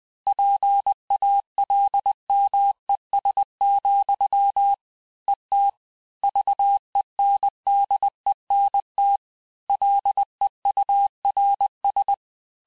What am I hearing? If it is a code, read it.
Morse code, heard as a keyed tone, about 20 wpm, 800 Hz. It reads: PALMES, ET VENDENT LEURS
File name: cours-du-mercredi / audios / cours-858-texte-clais-vitesse-19mpm